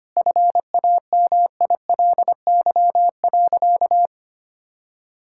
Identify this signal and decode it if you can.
Morse code, heard as a keyed tone, about 25 wpm, 700 Hz. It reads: FAMILY.
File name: practice / 25wpm / Difficult Sentences with Hints - 25wpm